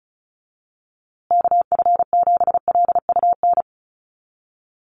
Morse code, heard as a keyed tone, about 34 wpm, 700 Hz. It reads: KF7LUN
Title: USA Call Signs Repeated 3x 35wpm